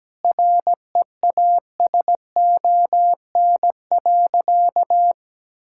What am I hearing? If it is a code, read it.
Morse code, heard as a keyed tone, about 17 wpm, 700 Hz. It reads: REASON.
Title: Sentences from Top 1600 Words 17wpm